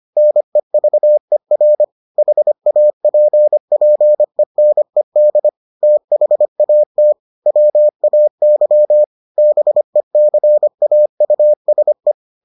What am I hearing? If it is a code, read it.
Morse code, heard as a keyed tone, about 25 wpm, 600 Hz. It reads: NEVER HAPPENED THAT WAY BECAUSE